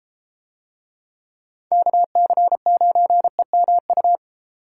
Morse code, heard as a keyed tone, about 33 wpm, 700 Hz. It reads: KC9EMU